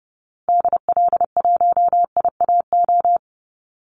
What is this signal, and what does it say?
Morse code, heard as a keyed tone, about 28 wpm, 700 Hz. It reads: DL1IAO
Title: Call Signs Repeated 2x - 30wpm